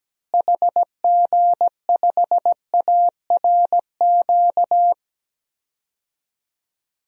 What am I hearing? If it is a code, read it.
Morse code, heard as a keyed tone, about 17 wpm, 700 Hz. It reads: HG5ARQ
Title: DXCC-100 Call Signs - 6 Characters - Repeated 2x - 17wpm